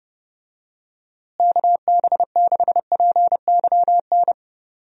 Morse code, heard as a keyed tone, about 31 wpm, 700 Hz. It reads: KB6PYN
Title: USA Call Signs - 6 Characters - Repeated 2x - 30wpm